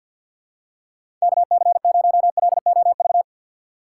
Morse code, heard as a keyed tone, about 46 wpm, 700 Hz. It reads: KC0LOV
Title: USA Call Signs - 3x Character Spacing - 50wpm